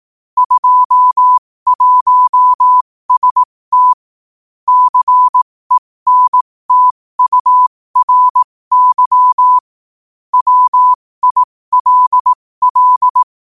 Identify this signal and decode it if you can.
Morse code, heard as a keyed tone, about 18 wpm, spaced as for 15 wpm, 1000 Hz. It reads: 21ST CENTURY WILL
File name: qotdfiles / 15wpm